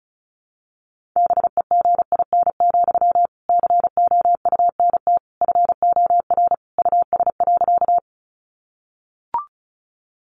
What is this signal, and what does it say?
Morse code, heard as a keyed tone, about 34 wpm, 700 Hz. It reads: BEGIN, COUNT FOR US.
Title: Sentences from Top 1500 Words 35wpm